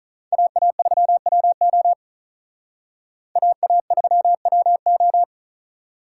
Morse code, heard as a keyed tone, about 45 wpm, 700 Hz. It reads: AA3WO AA3WO